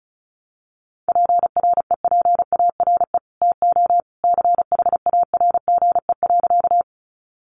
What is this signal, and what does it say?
Morse code, heard as a keyed tone, about 34 wpm, 700 Hz. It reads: PREPARE TO CHARGE.
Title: Sentences from Top 900 Words 35wpm